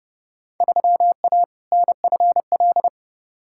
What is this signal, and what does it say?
Morse code, heard as a keyed tone, about 30 wpm, 700 Hz. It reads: 3A NFL
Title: Field Day Exchanges - 30wpm